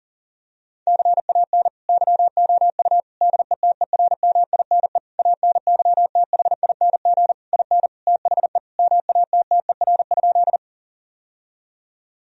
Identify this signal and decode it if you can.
Morse code, heard as a keyed tone, about 40 wpm, 700 Hz. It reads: CAN YOU DETERMINE ANYTHING IN THE MATTER?
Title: Sentences from Top 1000 Words - Review Part 4 - 40wpm